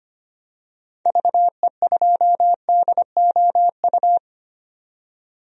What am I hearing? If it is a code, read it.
Morse code, heard as a keyed tone, about 25 wpm, 700 Hz. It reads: VE2DOU
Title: DXCC-100 Call Signs - 6 Characters - 25wpm